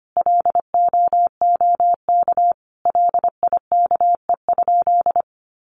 Morse code, heard as a keyed tone, about 25 wpm, 700 Hz. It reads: LOOK LIKE?
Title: Sentences from Top 100 Words 25wpm